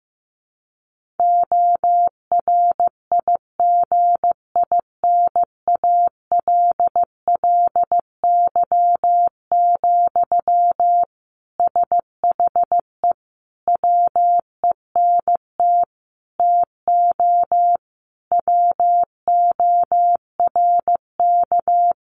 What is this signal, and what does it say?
Morse code, heard as a keyed tone, about 15 wpm, 700 Hz. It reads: ORIGINALLY, SHE WENT TO WORK
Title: Sentences from Top 1500 Words 15wpm